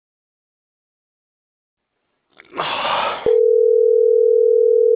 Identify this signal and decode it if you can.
Ringback tone (North America), 440 + 480 Hz.
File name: CENTRAL NAVIERAS / out onelinecapacitacion - .28002